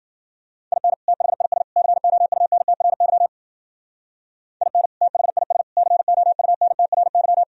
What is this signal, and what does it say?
Morse code, heard as a keyed tone, about 47 wpm, 700 Hz. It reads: IN THIS COUNTRY IN THIS COUNTRY
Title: Speed-Racing - Sets of 3 Words 40wpm